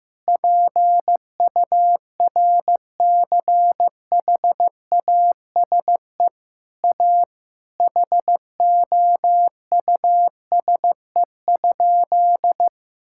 Morse code, heard as a keyed tone, about 15 wpm, 700 Hz. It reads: PURCHASE A HOUSE?
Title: Sentences from Top 1000 Words - Review Part 5 - 15wpm